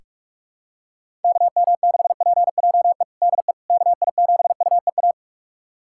Morse code, heard as a keyed tone, about 46 wpm, 700 Hz. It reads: KM6PJE DE KI7UEA
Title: USA Callsign DE Callsign 45wpm